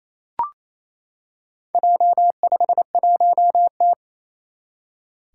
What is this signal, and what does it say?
Morse code, heard as a keyed tone, about 27 wpm, 700 Hz. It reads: J51T
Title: Four Challenging Letter-Number Combinations 28wpm